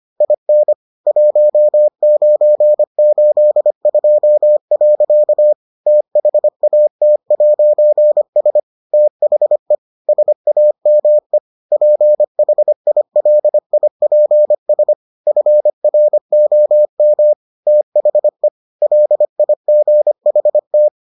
Morse code, heard as a keyed tone, about 25 wpm, 600 Hz. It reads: IN 1982. THAT'S THE SAME PHILIPS FROM THE LIGHT